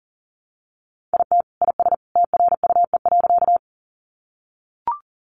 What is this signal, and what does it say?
Morse code, heard as a keyed tone, about 40 wpm, 700 Hz. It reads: IT IS TRUE.
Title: Sentences from Top 400 Words 40wpm